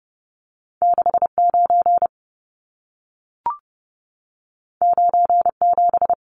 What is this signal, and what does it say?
Morse code, heard as a keyed tone, about 30 wpm, 700 Hz. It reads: 69 97
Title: Numbers - 2-Digit 30wpm